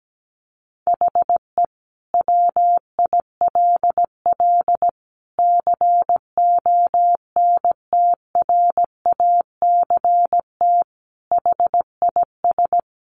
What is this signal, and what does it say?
Morse code, heard as a keyed tone, about 17 wpm, 700 Hz. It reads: HE WILL CONTRACT HIS
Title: Sentences from Top 1400 Words 17wpm